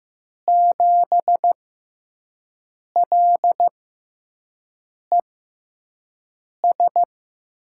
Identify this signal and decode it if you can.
Morse code, heard as a keyed tone, about 15 wpm, spaced as for 5 wpm, 700 Hz. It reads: 7LES
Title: USA Call Signs - 5 Characters - 6x Character Spacing - 15wpm